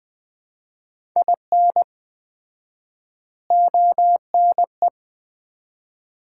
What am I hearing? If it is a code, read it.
Morse code, heard as a keyed tone, about 20 wpm, 700 Hz. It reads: IN ONE
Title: 4x Word-Spacing - Sets of 3 Words - 20wpm